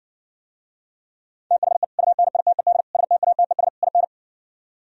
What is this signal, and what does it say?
Morse code, heard as a keyed tone, about 59 wpm, 700 Hz. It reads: THE UNITED STATES IN